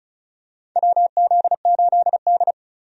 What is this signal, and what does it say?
Morse code, heard as a keyed tone, about 36 wpm, 700 Hz. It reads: WZ8D
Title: USA Call Signs - 4 Characters - 35wpm